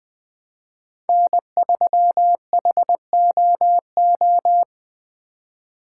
Morse code, heard as a keyed tone, about 20 wpm, 700 Hz. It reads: N3HOO
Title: USA Call Signs - 5 Characters - 3x Character Spacing - 20wpm